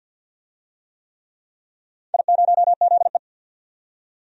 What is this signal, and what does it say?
Morse code, heard as a keyed tone, about 50 wpm, 700 Hz. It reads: I0ZE
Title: DXCC-100 Call Signs - 4 Characters - 50wpm